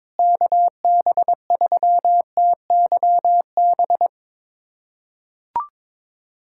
Morse code, heard as a keyed tone, about 22 wpm, 700 Hz. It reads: KB3TYB
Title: USA Callsign DE Callsign 22wpm